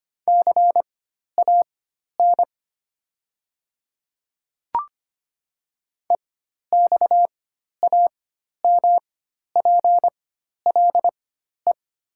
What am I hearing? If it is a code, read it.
Morse code, heard as a keyed tone, about 25 wpm, spaced as for 12 wpm, 700 Hz. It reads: CAN EXAMPLE
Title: 4x Character Spacing - Top 500 Words - 25wpm